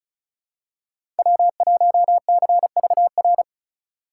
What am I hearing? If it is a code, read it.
Morse code, heard as a keyed tone, about 34 wpm, 700 Hz. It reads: W1CVR
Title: USA Call Signs - 5 Characters - 2x Character Spacing - 35wpm